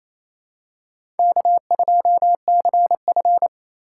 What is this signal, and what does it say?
Morse code, heard as a keyed tone, about 28 wpm, 700 Hz. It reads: K2CF
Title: USA Call Signs - 4 Characters - 3x Character Spacing - 28wpm